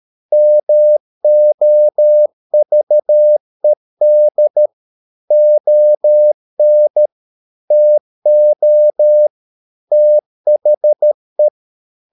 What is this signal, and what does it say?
Morse code, heard as a keyed tone, about 13 wpm, 600 Hz. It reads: MOVED ON TO THE